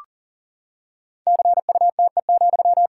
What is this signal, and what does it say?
Morse code, heard as a keyed tone, about 40 wpm, 700 Hz. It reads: CUTE,
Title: Pangram Challenge v2 - 40wpm